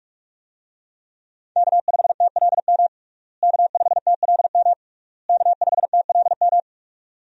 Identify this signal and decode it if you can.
Morse code, heard as a keyed tone, about 41 wpm, 700 Hz. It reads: K5TLM K5TLM K5TLM
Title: USA Call Signs - 5 Characters - Repeated 3x - 45wpm